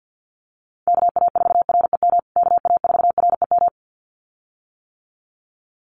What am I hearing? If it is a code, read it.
Morse code, heard as a keyed tone, about 50 wpm, 700 Hz. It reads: KA4REM KA4REM
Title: USA Call Signs Repeated 2x 50wpm